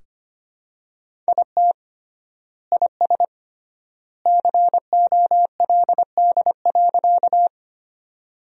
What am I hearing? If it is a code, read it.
Morse code, heard as a keyed tone, about 25 wpm, 700 Hz. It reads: IT IS COLD.
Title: 3x Word-Spacing - Sentences from Top 400 Words - 25wpm